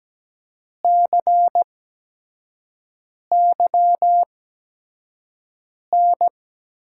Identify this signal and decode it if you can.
Morse code, heard as a keyed tone, about 17 wpm, spaced as for 5 wpm, 700 Hz. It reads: CYN
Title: USA Call Signs - 6 Characters - 8x Character Spacing - 17wpm